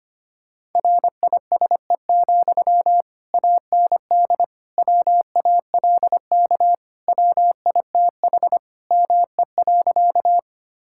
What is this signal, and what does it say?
Morse code, heard as a keyed tone, about 25 wpm, 700 Hz. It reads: RISE, AND WALK WITH ME.
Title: Sentences from Top 1000 Words - Review Part 6 - 25wpm